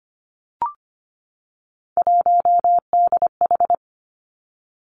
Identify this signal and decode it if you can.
Morse code, heard as a keyed tone, about 25 wpm, 700 Hz. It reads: 1DH
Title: Three Challenging Letter-Number Combinations 25wpm